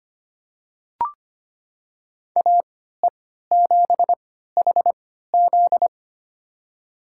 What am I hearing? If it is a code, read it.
Morse code, heard as a keyed tone, about 25 wpm, spaced as for 14 wpm, 700 Hz. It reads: AE7HZ
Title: USA Call Signs - 5 Characters - 3x Character Spacing - 25wpm